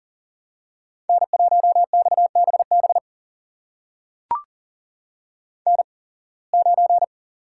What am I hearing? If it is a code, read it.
Morse code, heard as a keyed tone, about 40 wpm, spaced as for 26 wpm, 700 Hz. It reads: N1XBB N9
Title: USA Call Signs - 5 Characters - 8x Character Spacing - 40wpm